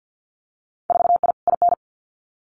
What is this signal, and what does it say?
Morse code, heard as a keyed tone, about 50 wpm, 700 Hz. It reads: 4I IN
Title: Winter Field Day Exchanges Encore 50wpm